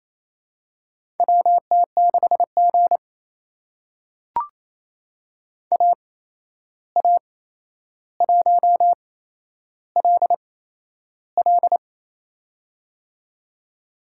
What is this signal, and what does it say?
Morse code, heard as a keyed tone, about 28 wpm, spaced as for 11 wpm, 700 Hz. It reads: WT6G AA1LL
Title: USA Call Signs - 8x Character Spacing - 28wpm